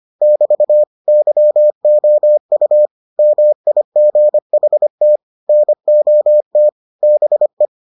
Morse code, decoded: <BT> YOU MIGHT NOT BE

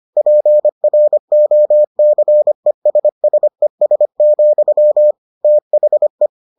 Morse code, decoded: PROCESSES, THE

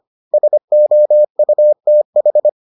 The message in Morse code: SOUTH

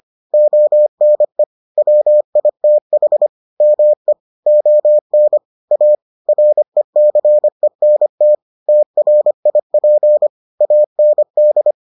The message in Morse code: ONE WITH ME ON A RECENT TRIP AND

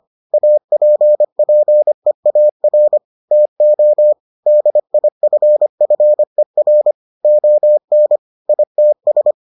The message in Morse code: APPEAR TO DIFFER ON ITS

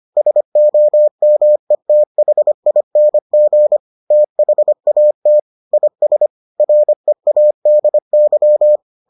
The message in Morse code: SOMETHING THAT IS READY

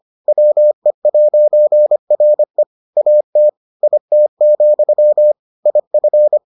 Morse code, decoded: WE'RE AT IT, IF

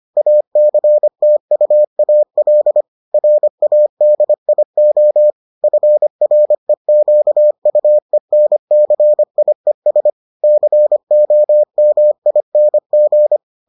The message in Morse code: ACTUAL RADIO FREQUENCIES COMING

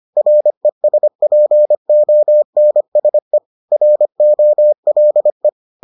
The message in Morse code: RESPONSE ROLE

